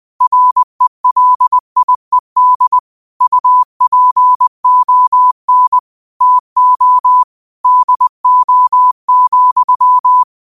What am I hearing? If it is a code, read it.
Morse code, heard as a keyed tone, about 20 wpm, 1000 Hz. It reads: RELIED UPON TO DO,
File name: qotdfiles / 20wpm